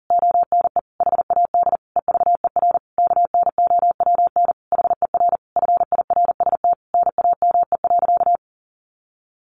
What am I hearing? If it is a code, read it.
Morse code, heard as a keyed tone, about 40 wpm, 700 Hz. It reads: ONE HAD EVER KNOWN HER FIRST NAME.
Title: Sentences from Top 1200 Words 40wpm